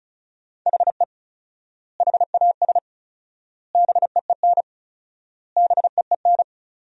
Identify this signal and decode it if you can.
Morse code, heard as a keyed tone, about 34 wpm, 700 Hz. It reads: HE HAS BEEN BEEN
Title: 4x Word-Spacing - Sentences from Top 300 Words - 35wpm